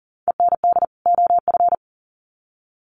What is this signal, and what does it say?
Morse code, heard as a keyed tone, about 40 wpm, 700 Hz. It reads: END OF
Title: Sets of 2 Words 40wpm